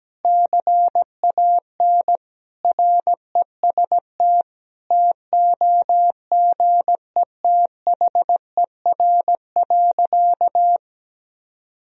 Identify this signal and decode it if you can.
Morse code, heard as a keyed tone, about 17 wpm, 700 Hz. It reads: CAN REST TOGETHER.
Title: Sentences from Top 500 Words and Abbreviations 17wpm